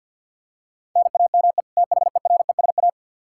Morse code, heard as a keyed tone, about 51 wpm, 700 Hz. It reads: NAME THERESA
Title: K1USN SST QSO Practice 50wpm